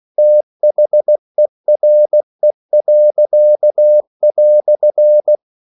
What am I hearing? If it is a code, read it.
Morse code, heard as a keyed tone, about 16 wpm, 600 Hz. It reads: THERE."